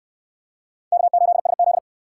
Morse code, heard as a keyed tone, about 63 wpm, 700 Hz. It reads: K8SZ